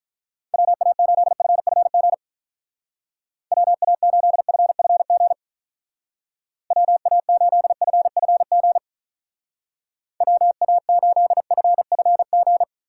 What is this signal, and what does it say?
Morse code, heard as a keyed tone, about 44 wpm, 700 Hz. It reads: WA8FFG WA8FFG WA8FFG WA8FFG